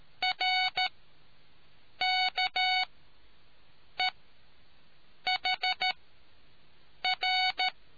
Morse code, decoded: RKEHR

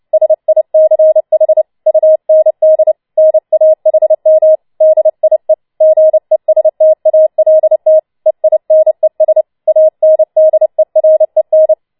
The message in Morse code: SICH UND NAHM DIE GESTALT EINES ANDEREN

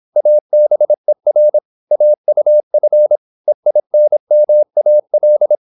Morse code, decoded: ABER AUF EINMAL

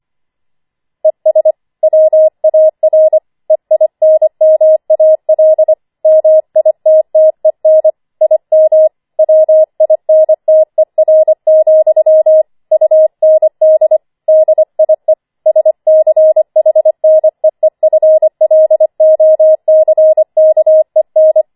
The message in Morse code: ES WAR EINMAL MITTEN IM WINTER, UND DIE SCHNEEFLOCKEN

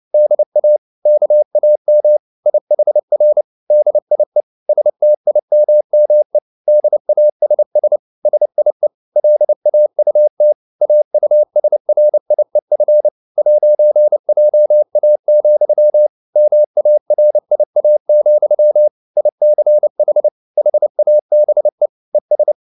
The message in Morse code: DA KAM IHR DIE STIMME DASS SIE LAUT AUSRIEF 'JA, MARIA, ICH HABE ES

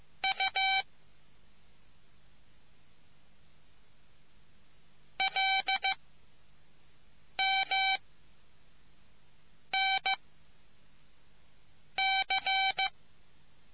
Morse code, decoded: U L M N C

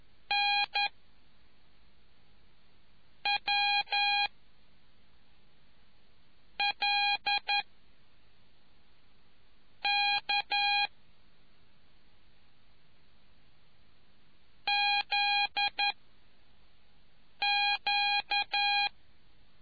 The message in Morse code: N W L K Z Q

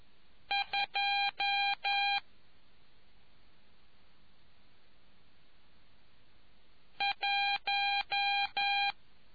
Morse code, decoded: 2 1

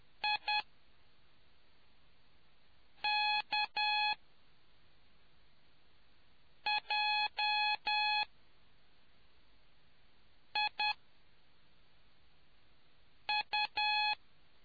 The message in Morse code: IKJIU

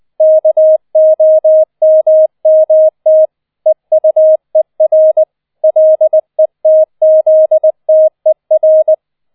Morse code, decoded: KOMMT EUER LETZTER

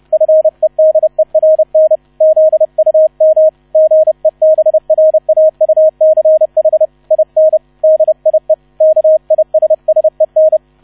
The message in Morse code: FEDERN ZUM GEBRAUCH IN DIE KISSEN

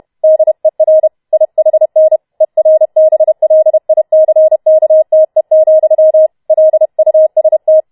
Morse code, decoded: DER IHN ERBLICKTE, LUST

